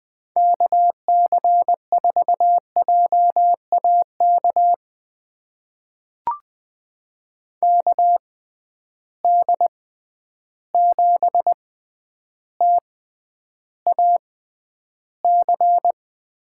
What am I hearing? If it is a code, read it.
Morse code, heard as a keyed tone, about 20 wpm, spaced as for 10 wpm, 700 Hz. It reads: KC4JAK KD7TAC
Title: USA Call Signs - 6 Characters - 6x Character Spacing - 20wpm